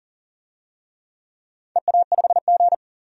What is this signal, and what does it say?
Morse code, heard as a keyed tone, about 40 wpm, 700 Hz. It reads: EA5G
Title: DXCC-100 Call Signs - 4 Characters - 40wpm